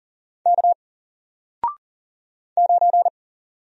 Morse code, heard as a keyed tone, about 40 wpm, 700 Hz. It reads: K 9
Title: Single Letter-Number 40wpm